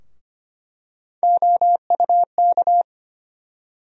Morse code, heard as a keyed tone, about 25 wpm, 700 Hz. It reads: OUK
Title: USA Call Sign Postfixes - 3 Letters 25wpm